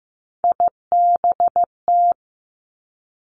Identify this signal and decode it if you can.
Morse code, heard as a keyed tone, about 15 wpm, 700 Hz. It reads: IBT
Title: Three Challenging Letter-Number Combinations 15wpm